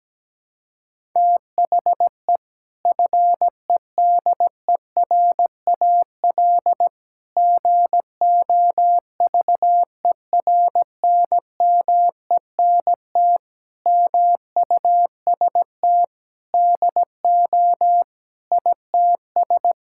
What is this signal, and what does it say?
Morse code, heard as a keyed tone, about 17 wpm, 700 Hz. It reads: THE FEDERAL GOVERNMENT MUST DO ITS
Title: Sentences from Top 1000 Words - Review Part 5 - 17wpm